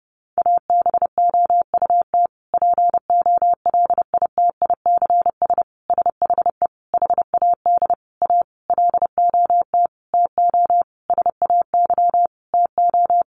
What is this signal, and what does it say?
Morse code, heard as a keyed tone, about 30 wpm, 700 Hz. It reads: ABOUT POLITICS SHE HAD A LOT TO SAY TO
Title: Sentences from Top 1000 Words 30wpm